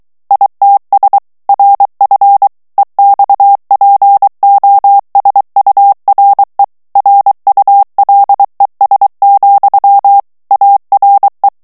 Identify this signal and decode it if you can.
Morse code, heard as a keyed tone, about 23 wpm, 800 Hz. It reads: ITS RF EXPOSURE RULES, ARE
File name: ANGLAIS texte clair 3 vitesse 22 mots minute